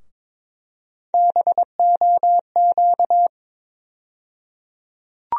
Morse code, heard as a keyed tone, about 22 wpm, 700 Hz. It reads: BOQ
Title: USA Call Sign Postfixes - 3 Letters 22wpm